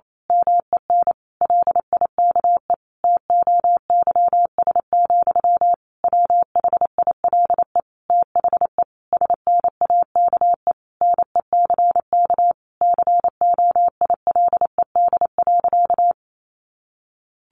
Morse code, heard as a keyed tone, about 28 wpm, 700 Hz. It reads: MEN LIKE TOYS, WHILE THE SNAKE NECK COILED.
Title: A Sound Of Thunder 28wpm